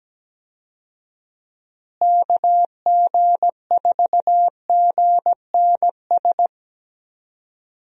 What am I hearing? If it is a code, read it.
Morse code, heard as a keyed tone, about 17 wpm, 700 Hz. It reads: KG4GNS
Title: USA Call Signs - 6 Characters - Repeated 2x - 17wpm